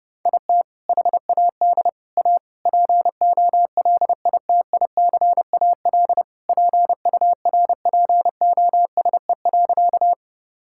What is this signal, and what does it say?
Morse code, heard as a keyed tone, about 31 wpm, 700 Hz. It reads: IT HAD A POLITICAL PURPOSE.
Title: Sentences from Top 1000 Words - Review Part 5 - 30wpm